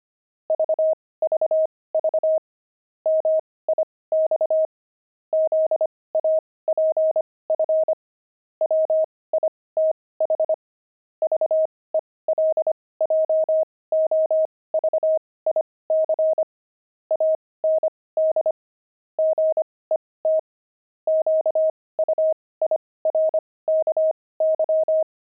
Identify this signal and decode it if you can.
Morse code, heard as a keyed tone, about 25 wpm, spaced as for 18 wpm, 630 Hz. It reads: VVV MIX ZAPF WITH VELJOVIC AND GET QUIRKY
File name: pangram-fs-18-cs-25-zaph